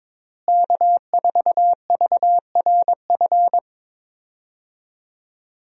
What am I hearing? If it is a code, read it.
Morse code, heard as a keyed tone, about 22 wpm, 700 Hz. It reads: K4VRF
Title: USA Call Signs - 5 Characters - Repeated 2x - 22wpm